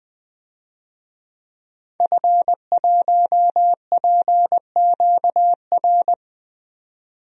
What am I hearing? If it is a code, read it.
Morse code, heard as a keyed tone, about 20 wpm, 700 Hz. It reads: F1PQR
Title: DXCC-100 Call Signs - 5 Characters - 20wpm